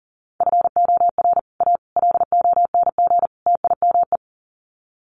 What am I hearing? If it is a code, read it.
Morse code, heard as a keyed tone, about 40 wpm, 700 Hz. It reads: FOR A LONG TIME
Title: Sets of 4 Words 40wpm